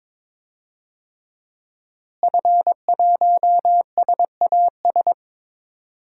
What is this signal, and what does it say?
Morse code, heard as a keyed tone, about 22 wpm, 700 Hz. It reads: F1SAS
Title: DXCC-100 Call Signs - 5 Characters - Repeated 3x - 22wpm